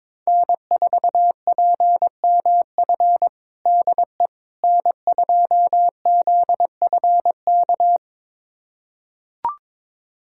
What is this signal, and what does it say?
Morse code, heard as a keyed tone, about 22 wpm, 700 Hz. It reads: N4PMF DE N2ZFK
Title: USA Callsign DE Callsign - 5 Characters - 22wpm